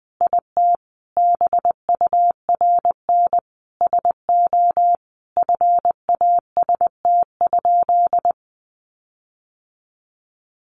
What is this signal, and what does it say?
Morse code, heard as a keyed tone, about 20 wpm, 700 Hz. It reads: IT BURN SO FAST?
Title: Sentences from Top 1300 Words 20wpm